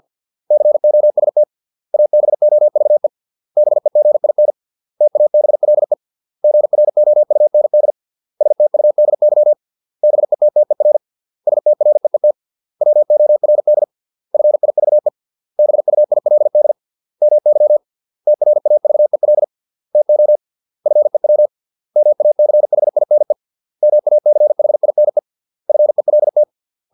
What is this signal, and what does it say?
Morse code, heard as a keyed tone, about 50 wpm, 600 Hz. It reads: COST ABOVE BEGIN TABLE GROUND STUDY BETTER STREET WORD FIVE BUILD MY TRAVEL TO FEW MACHINE MACHINE FELT